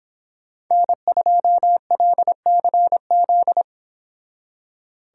Morse code, heard as a keyed tone, about 26 wpm, 700 Hz. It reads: N2LCZ
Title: Speed Racing - USA Call Signs 22wpm